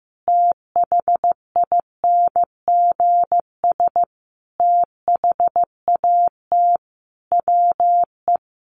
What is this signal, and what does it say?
Morse code, heard as a keyed tone, about 15 wpm, 700 Hz. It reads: THINGS THAT WE
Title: Sets of 5 Words Encore 15wpm